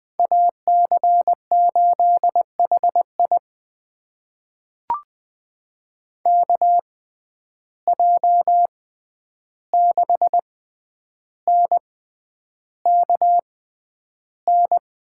Morse code, heard as a keyed tone, about 20 wpm, spaced as for 10 wpm, 700 Hz. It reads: AC8HI KJ6NKN
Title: USA Call Signs - 6x Character Spacing - 20wpm